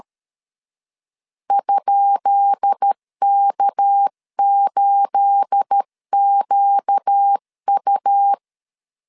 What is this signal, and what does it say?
Morse code, heard as a keyed tone, about 13 wpm, 800 Hz. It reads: ?K8QU
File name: LEZIONI / 10 SIGNS / 10B INT 12wpm 2414